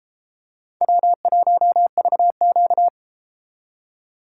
Morse code, heard as a keyed tone, about 31 wpm, 700 Hz. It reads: W1VQ